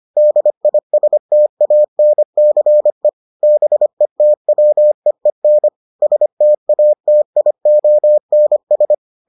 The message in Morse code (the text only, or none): DISTANCE BETWEEN STATIONS